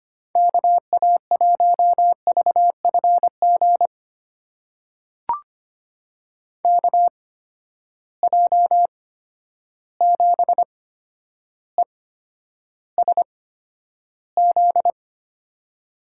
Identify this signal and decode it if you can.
Morse code, heard as a keyed tone, about 25 wpm, spaced as for 11 wpm, 700 Hz. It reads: KA1VFG KJ7ESZ
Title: USA Call Signs - 6 Characters - 8x Character Spacing - 25wpm